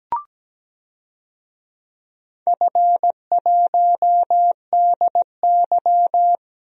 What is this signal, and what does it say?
Morse code, heard as a keyed tone, about 17 wpm, 700 Hz. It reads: F1DY
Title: DXCC-100 Call Signs - 4 Characters - 17wpm